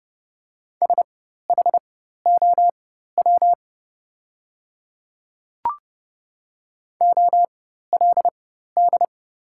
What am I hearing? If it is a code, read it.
Morse code, heard as a keyed tone, about 30 wpm, spaced as for 14 wpm, 700 Hz. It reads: SHOW OLD
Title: 4x Character Spacing - Top 200 Words - 30wpm